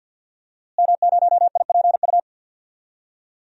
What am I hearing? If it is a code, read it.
Morse code, heard as a keyed tone, about 51 wpm, 700 Hz. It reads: M0IPU
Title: DXCC-100 Call Signs - 5 Characters - Repeated 2x - 50wpm